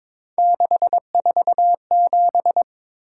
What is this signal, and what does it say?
Morse code, heard as a keyed tone, about 23 wpm, 700 Hz. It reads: 647
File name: practice / 22wpm / Three Letter-Number Combinations - 22wpm